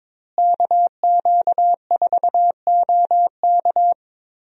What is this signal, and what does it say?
Morse code, heard as a keyed tone, about 21 wpm, 700 Hz. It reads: KQ4OK
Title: USA Call Signs - 5 Characters - Repeated 3x - 22wpm